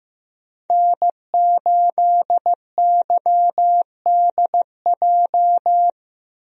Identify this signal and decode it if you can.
Morse code, heard as a keyed tone, about 15 wpm, 700 Hz. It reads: N8YDJ
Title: USA Call Signs - 5 Characters - Repeated 3x - 15wpm